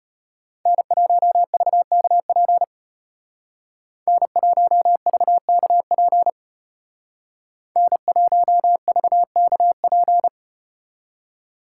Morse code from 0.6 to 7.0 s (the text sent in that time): N1VKP N1VKP